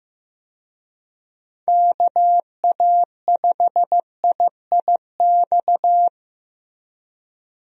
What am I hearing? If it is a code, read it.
Morse code, heard as a keyed tone, about 15 wpm, 700 Hz. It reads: KA5IIX
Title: USA Call Signs - 6 Characters - Repeated 3x - 15wpm